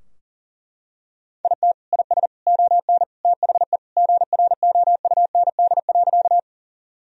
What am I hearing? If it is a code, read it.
Morse code, heard as a keyed tone, about 40 wpm, 700 Hz. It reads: IT IS ON THE GROUND.
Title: Sentences from Top 1000 Words - Review Part 2 - 40wpm